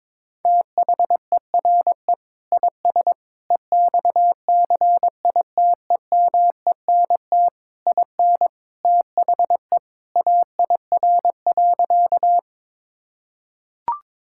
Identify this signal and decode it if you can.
Morse code, heard as a keyed tone, about 22 wpm, 700 Hz. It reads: THERE IS EXCITEMENT IN THE AIR.
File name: practice / 22wpm / Sentences with 3-Word Phrases - 22wpm